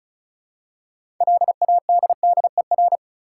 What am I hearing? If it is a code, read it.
Morse code, heard as a keyed tone, about 34 wpm, 700 Hz. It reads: LADDER
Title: Six Letter Words - 35wpm